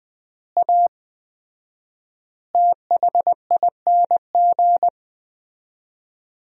Morse code, heard as a keyed tone, about 20 wpm, 700 Hz. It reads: A THING
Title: 4x Word-Spacing - Sentences from Top 300 Words - 20wpm